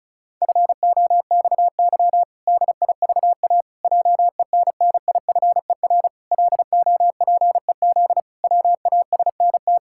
Morse code, heard as a keyed tone, about 34 wpm, 700 Hz. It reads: FOXY DIVA JENNIFER LOPEZ WASNT
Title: Pangram Challenge v2 - 35wpm